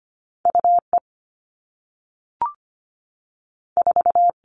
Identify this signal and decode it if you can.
Morse code, heard as a keyed tone, about 25 wpm, 700 Hz. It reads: UE 4
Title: All Cut Numbers - 2-Digit 25wpm